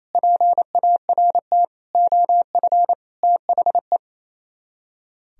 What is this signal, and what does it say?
Morse code, heard as a keyed tone, about 34 wpm, 700 Hz. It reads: PART OF THE
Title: Speed-Racing - Sets of 5 Words 28wpm